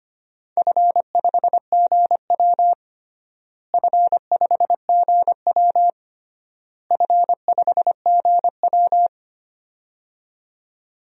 Morse code, heard as a keyed tone, about 25 wpm, 700 Hz. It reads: F5GW F5GW F5GW
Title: DXCC-100 Call Signs - 4 Characters - Repeated 3x - 25wpm